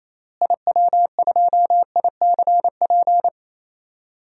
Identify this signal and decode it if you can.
Morse code, heard as a keyed tone, about 28 wpm, 700 Hz. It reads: IW2ICP
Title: DXCC-100 Call Signs - 6 Characters - Repeated 2x - 28wpm